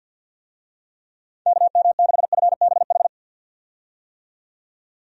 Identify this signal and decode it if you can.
Morse code, heard as a keyed tone, about 49 wpm, 700 Hz. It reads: KM6FBH
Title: USA Call Signs - 6 Characters - 50wpm